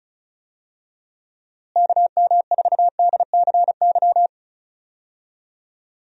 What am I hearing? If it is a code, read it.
Morse code, heard as a keyed tone, about 36 wpm, 700 Hz. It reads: KM4DCY
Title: USA Call Signs - 6 Characters - Repeated 2x - 35wpm